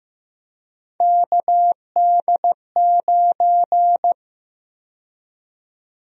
Morse code, heard as a keyed tone, about 15 wpm, 700 Hz. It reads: KD9
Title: USA Call Sign Prefixes - Repeated 2x - 15wpm